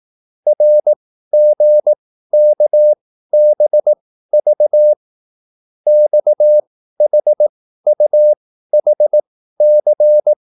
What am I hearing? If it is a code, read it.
Morse code, heard as a keyed tone, about 18 wpm, spaced as for 13 wpm, 600 Hz. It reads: RGKBV XHUHC
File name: QSL / CW lesson